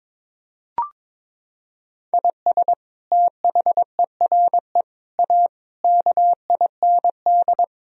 Morse code, heard as a keyed tone, about 22 wpm, 700 Hz. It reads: IS THERE A KIND
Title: Sentences from Top 1000 Words - Review Part 3 - 22wpm